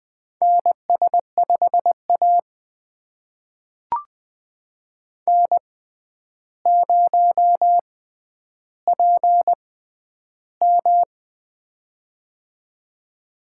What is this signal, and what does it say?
Morse code, heard as a keyed tone, about 20 wpm, spaced as for 10 wpm, 700 Hz. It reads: NS5A N0PM
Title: USA Call Signs - 4 Characters - 6x Character Spacing - 20wpm